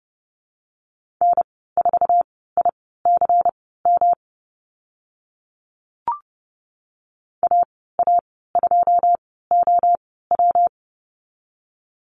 Morse code, heard as a keyed tone, about 30 wpm, spaced as for 17 wpm, 700 Hz. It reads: N4ICM AA2OW
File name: Lesson 19g - Letter w - Call Signs 30wpm